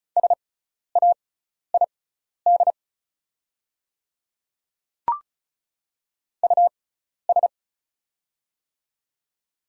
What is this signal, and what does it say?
Morse code, heard as a keyed tone, about 34 wpm, spaced as for 12 wpm, 700 Hz. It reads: SAID US
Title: 6x Character Spacing - Top 200 Words - 35wpm